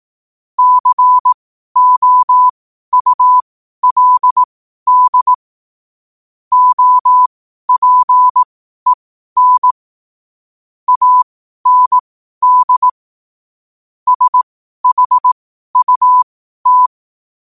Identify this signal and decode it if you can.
Morse code, heard as a keyed tone, about 18 wpm, spaced as for 12 wpm, 1000 Hz. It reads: COULD OPEN AND SHUT
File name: qotdfiles / 12wpm